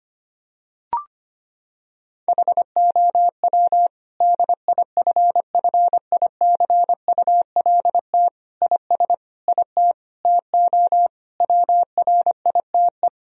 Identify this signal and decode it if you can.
Morse code, heard as a keyed tone, about 25 wpm, 700 Hz. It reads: HOW DIFFICULT IS IT TO WRITE
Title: Sentences from Top 1000 Words - Review Part 3 - 25wpm